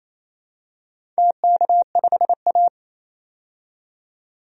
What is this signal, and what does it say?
Morse code, heard as a keyed tone, about 28 wpm, 700 Hz. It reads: TK5A
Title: DXCC-100 Call Signs - 4 Characters - 28wpm